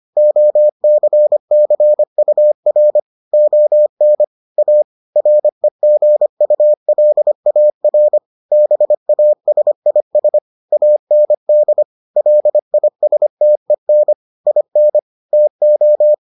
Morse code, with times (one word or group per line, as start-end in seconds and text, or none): 0.2-3.0 OCCUR
3.3-4.2 ON
4.6-4.8 A
5.2-8.2 REGULAR
8.5-10.4 BASIS
10.7-11.8 AND
12.2-14.1 LISTEN
14.5-15.0 IN
15.3-16.1 TO